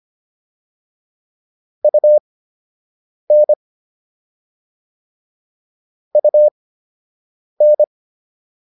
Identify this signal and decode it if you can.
Morse code, heard as a keyed tone, about 25 wpm, spaced as for 7 wpm, 600 Hz. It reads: UN UN